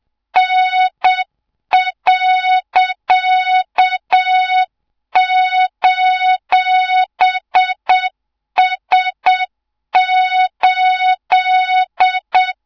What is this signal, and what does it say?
Morse code, heard as a keyed tone, about 7 wpm, 3700 Hz, with noise. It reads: N.:S8
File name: Morze-HA2MN / hang / 24 GYAKORLAT